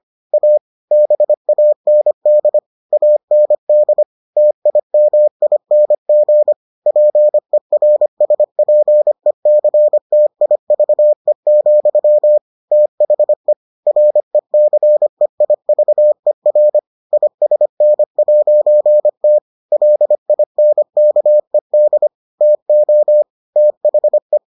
Morse code, heard as a keyed tone, about 25 wpm, 600 Hz. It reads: A BAND AND TIMING PERSPECTIVE, THE RECEIVER ISN'T LINKED TO THE